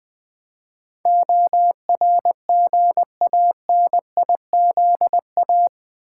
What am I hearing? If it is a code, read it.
Morse code, heard as a keyed tone, about 20 wpm, 700 Hz. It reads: ORGANIZA
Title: Build A Word - 4 Letters At A Time - 12-Letter Words - 20wpm